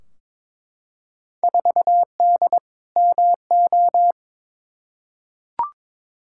Morse code, heard as a keyed tone, about 21 wpm, 700 Hz. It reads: 4D MO
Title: Field Day Exchanges Encore - 22wpm